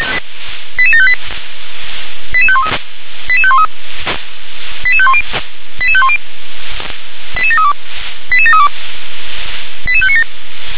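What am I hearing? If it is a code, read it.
Five-tone selective call (ZVEI), heard as quick runs of five tones, tones of about 70 ms. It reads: 88758 88513 88513 88510 88510 88732 88732 88597
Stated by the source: ZVEI five-tone selective call received on an SDR.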